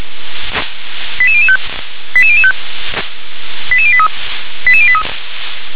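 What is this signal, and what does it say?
Five-tone selective call (ZVEI), heard as quick runs of five tones, tones of about 80 ms. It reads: 88005 88005 88083 88083
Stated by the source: ZVEI five-tone selective call received on an SDR.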